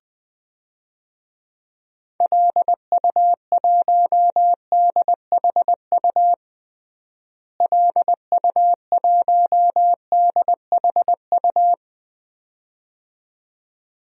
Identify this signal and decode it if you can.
Morse code, heard as a keyed tone, about 20 wpm, 700 Hz. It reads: LU1DHU LU1DHU
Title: DXCC-100 Call Signs - 6 Characters - Repeated 2x - 20wpm